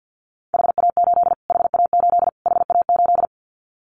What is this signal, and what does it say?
Morse code, heard as a keyed tone, about 50 wpm, 700 Hz. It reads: HA8 HA8 HA8
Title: Callsign Prefixes for Top 25 DXCC - Repeated 3x - 50wpm